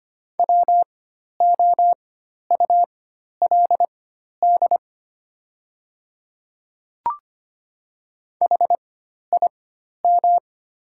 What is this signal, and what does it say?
Morse code, heard as a keyed tone, about 25 wpm, spaced as for 12 wpm, 700 Hz. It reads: WOULD HIM
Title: 4x Character Spacing - Top 100 Words - 25wpm